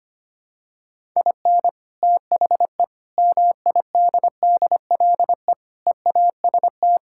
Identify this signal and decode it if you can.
Morse code, heard as a keyed tone, about 25 wpm, 700 Hz. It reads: IN THE MIDDLE EAST
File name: practice / 25wpm / Sentences with 4-Word Phrases - 25wpm